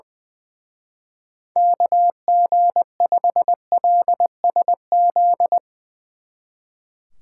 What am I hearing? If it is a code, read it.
Morse code, heard as a keyed tone, about 22 wpm, 700 Hz. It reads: KG5LSZ
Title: Speed Racing - USA 2x3 Call Signs 20wpm